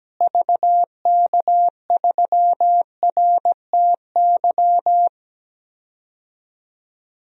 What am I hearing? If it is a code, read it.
Morse code, heard as a keyed tone, about 17 wpm, 700 Hz. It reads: VK3RTY
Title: DXCC-100 Call Signs - 6 Characters - Repeated 3x - 17wpm